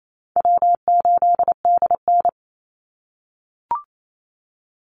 Morse code, heard as a keyed tone, about 28 wpm, 700 Hz. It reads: W8DN
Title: Call Signs - 28wpm